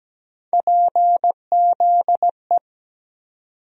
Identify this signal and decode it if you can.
Morse code, heard as a keyed tone, about 17 wpm, 700 Hz. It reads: PZE
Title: USA Call Sign Postfixes - 3 Letters 17wpm